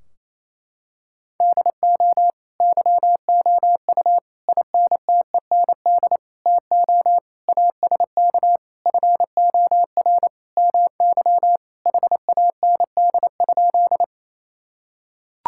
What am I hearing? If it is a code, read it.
Morse code, heard as a keyed tone, about 28 wpm, 700 Hz. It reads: DO YOU INTEND TO ASK FOR MY HAND?
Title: Sentences from Top 1600 Words 28wpm